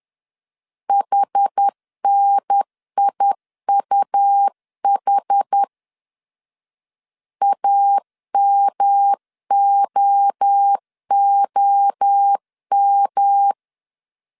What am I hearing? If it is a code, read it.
Morse code, heard as a keyed tone, about 10 wpm, 800 Hz. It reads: HNIUH AMOOM
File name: 10wpm